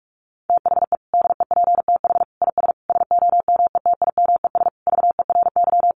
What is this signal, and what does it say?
Morse code, heard as a keyed tone, about 44 wpm, 700 Hz. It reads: THE DEPTH IS SOMETIMES VERY
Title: Sentences from Top 1200 Words 45wpm